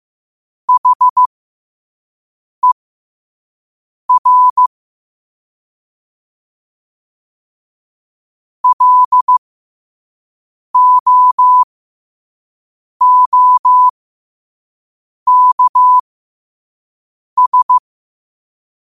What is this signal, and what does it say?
Morse code, heard as a keyed tone, about 15 wpm, spaced as for 5 wpm, 1000 Hz. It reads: HER LOOKS
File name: qotdfiles / 5wpm